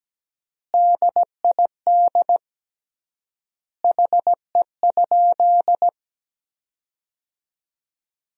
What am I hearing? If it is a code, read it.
Morse code, heard as a keyed tone, about 17 wpm, 700 Hz. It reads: DID HE?
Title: 3x Word-Spacing - Sentences from Top 200 Words - 17wpm